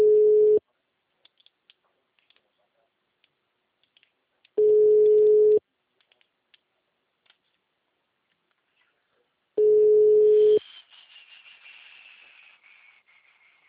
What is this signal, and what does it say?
Ringback tone (Europe, 425 Hz), with speech.